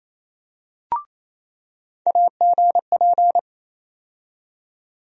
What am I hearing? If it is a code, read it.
Morse code, heard as a keyed tone, about 28 wpm, 700 Hz. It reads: AGP
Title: USA Call Sign Postfixes - 3 Letters 28wpm